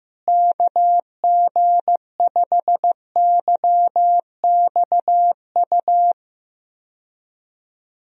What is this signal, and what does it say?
Morse code, heard as a keyed tone, about 16 wpm, 700 Hz. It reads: KG5YXU
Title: USA Call Signs - 6 Characters - Repeated 3x - 15wpm